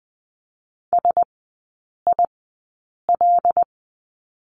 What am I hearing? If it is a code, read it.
Morse code, heard as a keyed tone, about 20 wpm, spaced as for 8 wpm, 700 Hz. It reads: SIL
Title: 127 Lesson 21e - Number 6 - Three Characters 20wpm